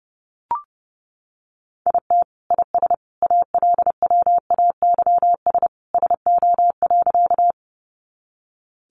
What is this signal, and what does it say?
Morse code, heard as a keyed tone, about 30 wpm, 700 Hz. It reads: IT IS ALWAYS SO.
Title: Sentences from Top 300 Words 30wpm